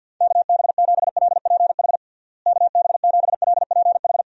Morse code, heard as a keyed tone, about 52 wpm, 700 Hz. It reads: KB7LPH KB7LPH
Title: USA Call Signs - 6 Characters - Repeated 3x - 50wpm